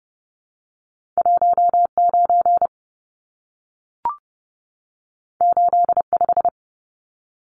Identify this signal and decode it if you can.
Morse code, heard as a keyed tone, about 30 wpm, 700 Hz. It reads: 19 85
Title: Numbers - 2-Digit 30wpm